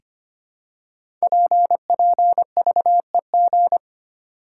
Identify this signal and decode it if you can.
Morse code, heard as a keyed tone, about 25 wpm, 700 Hz. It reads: PPVEG
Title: Code Groups - 5 Letter:Numbers - 25wpm